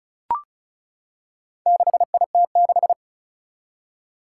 Morse code, heard as a keyed tone, about 36 wpm, 700 Hz. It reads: 6IT6
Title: Four Challenging Letter-Number Combinations 35wpm